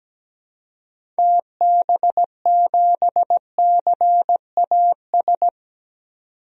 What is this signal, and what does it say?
Morse code, heard as a keyed tone, about 17 wpm, 700 Hz. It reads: TB7CAS
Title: DXCC-100 Call Signs - 6 Characters - 17wpm